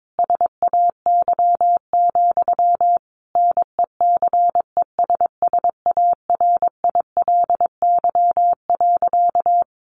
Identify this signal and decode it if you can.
Morse code, heard as a keyed tone, about 22 wpm, 700 Hz. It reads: SAY, NECESSARILY.
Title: Sentences from Top 1400 Words 22wpm